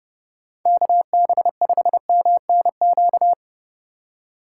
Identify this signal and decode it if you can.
Morse code, heard as a keyed tone, about 34 wpm, 700 Hz. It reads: KB5MNQ